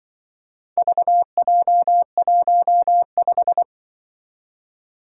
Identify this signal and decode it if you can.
Morse code, heard as a keyed tone, about 25 wpm, 700 Hz. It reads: VJ15
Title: Speed-Racing - Four Letter-Number Combinations - 20wpm